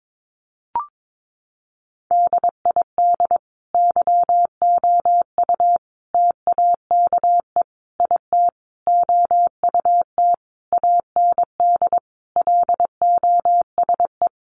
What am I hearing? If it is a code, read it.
Morse code, heard as a keyed tone, about 22 wpm, 700 Hz. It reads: DID YOU TAKE IT OUT AND LOSE
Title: Sentences from Top 1100 Words 22wpm